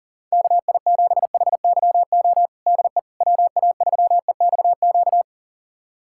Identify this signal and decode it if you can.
Morse code, heard as a keyed tone, about 37 wpm, 700 Hz. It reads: KI7HYO DE WA3EXQ
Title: USA Callsign DE Callsign - 6 Characters - 40wpm